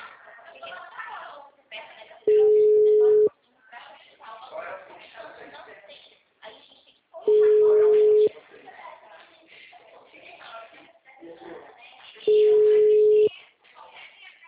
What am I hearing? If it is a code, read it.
Ringback tone (Europe, 425 Hz), with speech.